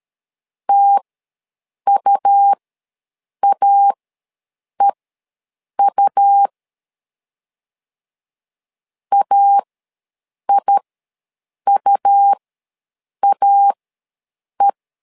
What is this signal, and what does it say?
Morse code, heard as a keyed tone, about 13 wpm, spaced as for 7 wpm, 800 Hz. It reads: TUAEU AIUAE
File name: LEZIONI / 2 UHDN / 2 U ws 12wpm 1655